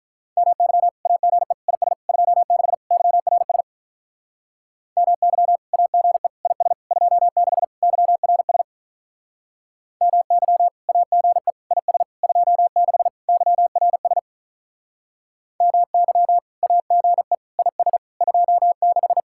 Morse code, heard as a keyed tone, about 44 wpm, 700 Hz. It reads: MY AGE IS 26 YRS MY AGE IS 26 YRS MY AGE IS 26 YRS MY AGE IS 26